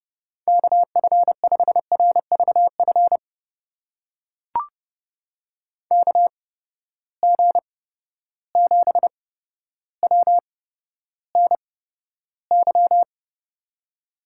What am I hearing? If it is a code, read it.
Morse code, heard as a keyed tone, about 30 wpm, spaced as for 13 wpm, 700 Hz. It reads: KF5RVF KG7WNY
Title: USA Call Signs - 6 Characters - 8x Character Spacing - 30wpm